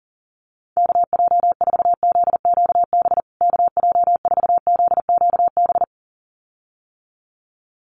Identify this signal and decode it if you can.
Morse code, heard as a keyed tone, about 40 wpm, 700 Hz. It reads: KJ4ZQB KJ4ZQB
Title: USA Call Signs Repeated 2x 40wpm